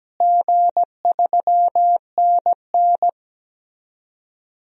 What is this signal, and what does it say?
Morse code, heard as a keyed tone, about 17 wpm, 700 Hz. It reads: G3NN
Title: DXCC-100 Call Signs - 4 Characters - Repeated 2x - 17wpm